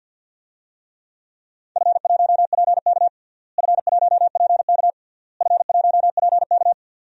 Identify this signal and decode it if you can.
Morse code, heard as a keyed tone, about 50 wpm, 700 Hz. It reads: F1PK F1PK F1PK
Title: DXCC-100 Call Signs - 4 Characters - Repeated 3x - 50wpm